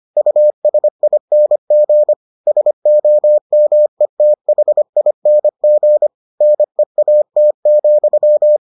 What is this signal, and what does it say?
Morse code, heard as a keyed tone, about 25 wpm, 600 Hz. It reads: USING SOMETHING NEAT,